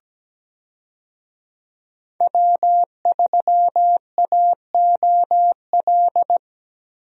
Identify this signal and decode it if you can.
Morse code, heard as a keyed tone, about 17 wpm, 700 Hz. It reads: W3AOL
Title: USA Call Signs - 5 Characters - Repeated 2x - 17wpm